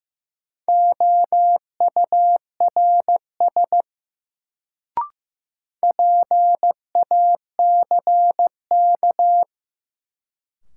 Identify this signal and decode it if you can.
Morse code, heard as a keyed tone, about 15 wpm, 700 Hz. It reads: OURS PACK
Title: Sentences from Top 1300 Words - New Word Drill Alt - 15wpm